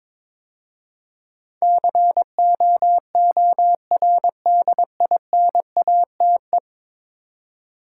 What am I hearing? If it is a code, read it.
Morse code, heard as a keyed tone, about 22 wpm, 700 Hz. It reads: COORDINATE
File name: practice / 22wpm / 10 Letter Words - First 3 Letters Given - 22wpm